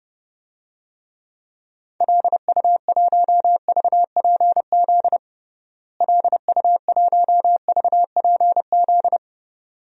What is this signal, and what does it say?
Morse code, heard as a keyed tone, about 30 wpm, 700 Hz. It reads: LU1VPZ LU1VPZ
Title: DXCC-100 Call Signs - 6 Characters - Repeated 2x - 30wpm